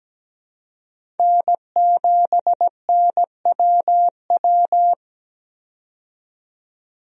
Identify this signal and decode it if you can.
Morse code, heard as a keyed tone, about 17 wpm, 700 Hz. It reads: N7NWW
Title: USA Call Signs - 5 Characters - 17wpm